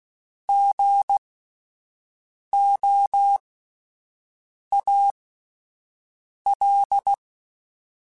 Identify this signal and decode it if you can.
Morse code, heard as a keyed tone, about 16 wpm, spaced as for 5 wpm, 800 Hz. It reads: GOAL
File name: Spaced Alphas / Spaced Alphas mp3s / Gs sp @10